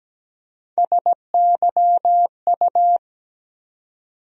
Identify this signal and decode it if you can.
Morse code, heard as a keyed tone, about 17 wpm, 700 Hz. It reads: SYU
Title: USA Call Sign Postfixes - 3 Letters 17wpm